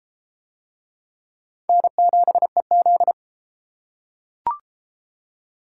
Morse code, heard as a keyed tone, about 29 wpm, 700 Hz. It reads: N7EZ